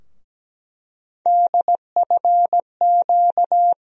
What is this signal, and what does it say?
Morse code, heard as a keyed tone, about 17 wpm, 700 Hz. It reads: DFQ
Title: USA Call Sign Postfixes - 3 Letters 17wpm